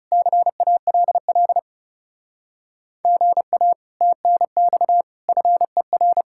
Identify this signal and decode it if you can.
Morse code, heard as a keyed tone, about 37 wpm, 700 Hz. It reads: CALL GA TNX FER
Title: Speed-Racing Common QSO Phrases - 30wpm